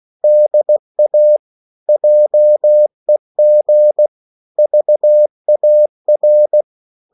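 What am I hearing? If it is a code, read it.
Morse code, heard as a keyed tone, about 16 wpm, 600 Hz. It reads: DA JEG VAR
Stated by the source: Slow (QRS) Morse code at 16 wpm, from CW med Gnister.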